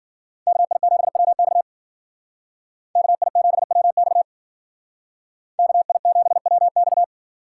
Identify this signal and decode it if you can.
Morse code, heard as a keyed tone, about 52 wpm, 700 Hz. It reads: KI7WX KI7WX KI7WX